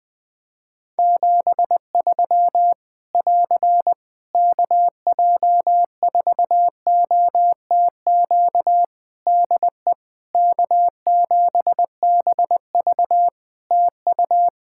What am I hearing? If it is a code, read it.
Morse code, heard as a keyed tone, about 20 wpm, 700 Hz. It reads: 73 <AR> KJ4OTQ DE K7BV TU
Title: K1USN SST QSO Practice 20wpm